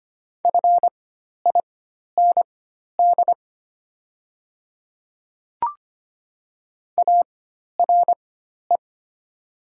Morse code, heard as a keyed tone, about 25 wpm, spaced as for 12 wpm, 700 Hz. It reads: FIND ARE
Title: 4x Character Spacing - Top 200 Words - 25wpm